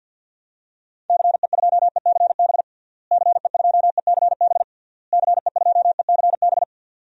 Morse code, heard as a keyed tone, about 48 wpm, 700 Hz. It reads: CE2ECB CE2ECB CE2ECB
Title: DXCC-100 Call Signs - 6 Characters - Repeated 3x - 50wpm